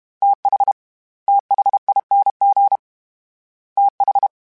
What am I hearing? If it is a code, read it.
Morse code, heard as a keyed tone, about 32 wpm, 800 Hz. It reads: TH THING TH
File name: TH / TH mp3s / TH Prefixes 1x1 @30